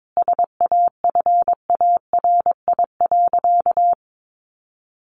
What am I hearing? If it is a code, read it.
Morse code, heard as a keyed tone, about 22 wpm, 700 Hz. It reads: SAFARI.
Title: A Sound Of Thunder 22wpm